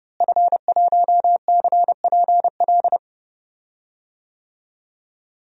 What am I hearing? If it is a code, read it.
Morse code, heard as a keyed tone, about 32 wpm, 700 Hz. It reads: F1CPL
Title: DXCC-100 Call Signs - 5 Characters - Repeated 2x - 30wpm